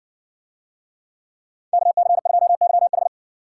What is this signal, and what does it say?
Morse code, heard as a keyed tone, about 55 wpm, 700 Hz. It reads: KC2YH